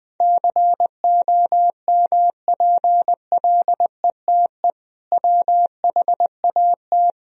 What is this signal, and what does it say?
Morse code, heard as a keyed tone, about 20 wpm, 700 Hz. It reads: COMPLETE WHAT
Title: Sentences from Top 1000 Words - Review Part 2 - 20wpm